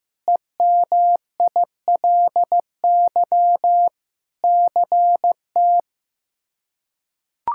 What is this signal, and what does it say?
Morse code, heard as a keyed tone, about 15 wpm, 700 Hz. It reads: EMILY CT
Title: K1USN SST QSO Practice - 15wpm